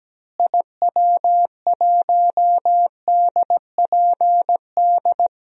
Morse code, heard as a keyed tone, about 17 wpm, 700 Hz. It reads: IW1DPD
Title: DXCC-100 Call Signs - 6 Characters - Repeated 2x - 17wpm